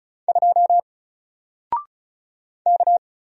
Morse code, heard as a keyed tone, about 36 wpm, 700 Hz. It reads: 2 K
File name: Single-Letter-Number-35wpm